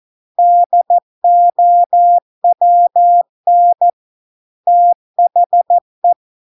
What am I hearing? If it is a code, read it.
Morse code, heard as a keyed tone, about 14 wpm, 700 Hz. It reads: DOWN THE